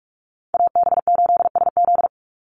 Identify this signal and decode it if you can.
Morse code, heard as a keyed tone, about 44 wpm, 700 Hz. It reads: AB8SZ
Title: USA Call Signs 45wpm